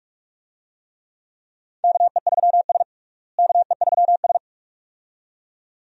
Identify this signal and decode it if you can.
Morse code, heard as a keyed tone, about 46 wpm, 700 Hz. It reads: KE3S KE3S
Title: USA Call Signs - 4 Characters - Repeated 2x - 45wpm